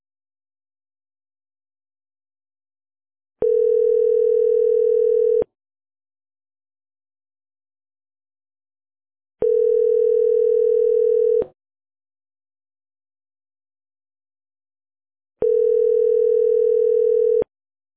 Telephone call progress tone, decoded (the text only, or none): ringback tone